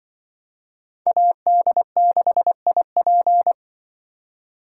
Morse code, heard as a keyed tone, about 24 wpm, 700 Hz. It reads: AD6IP
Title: Speed Racing - USA 2x2 Call Signs 20wpm